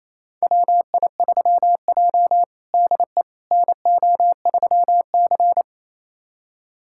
Morse code, heard as a keyed tone, about 28 wpm, 690 Hz. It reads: WI3J DE NO3C
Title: USA Callsign DE Callsign - 4 Characters - 28wpm